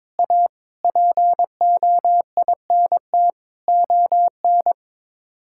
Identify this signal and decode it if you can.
Morse code, heard as a keyed tone, about 21 wpm, 700 Hz. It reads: A POINT ON
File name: practice / 22wpm / Sets of 8 Words - 22wpm